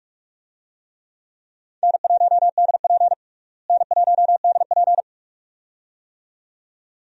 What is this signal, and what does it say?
Morse code, heard as a keyed tone, about 46 wpm, 700 Hz. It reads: N1DP N1DP
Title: USA Call Signs - 4 Characters - Repeated 2x - 45wpm